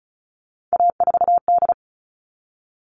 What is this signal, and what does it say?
Morse code, heard as a keyed tone, about 34 wpm, 700 Hz. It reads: A4D
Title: All Cut Numbers - 3-Digit 35wpm